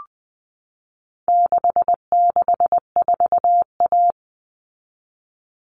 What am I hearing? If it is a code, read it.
Morse code, heard as a keyed tone, about 20 wpm, 700 Hz. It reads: 664A
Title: All Cut Numbers - 4-Digit 20wpm